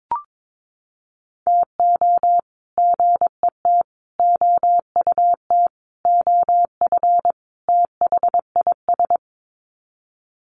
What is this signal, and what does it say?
Morse code, heard as a keyed tone, about 22 wpm, 700 Hz. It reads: TO GET OUT OF THIS
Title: Sets of 5 Words Encore2 22wpm